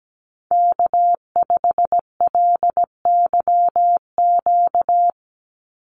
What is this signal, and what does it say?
Morse code, heard as a keyed tone, about 17 wpm, 700 Hz. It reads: K5LYQ
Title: USA Call Signs 17wpm